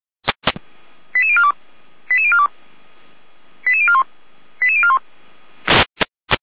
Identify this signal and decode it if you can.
Five-tone selective call (ZVEI), heard as quick runs of five tones, tones of about 70 ms. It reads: 88042 88042 88041 88041